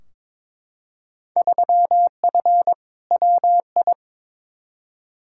Morse code, heard as a keyed tone, about 22 wpm, 700 Hz. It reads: 3F WI
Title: Field Day Exchanges - 22wpm